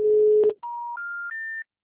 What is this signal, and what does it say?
Special information tone.